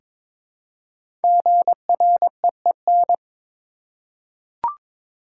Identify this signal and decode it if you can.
Morse code, heard as a keyed tone, about 23 wpm, 700 Hz. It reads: GREEN
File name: 5 Letter Words - First 3 Letters Given - 22wpm